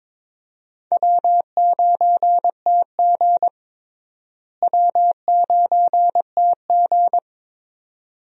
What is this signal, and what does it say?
Morse code, heard as a keyed tone, about 22 wpm, 700 Hz. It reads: W9TG W9TG
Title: USA Call Signs - 4 Characters - Repeated 2x - 22wpm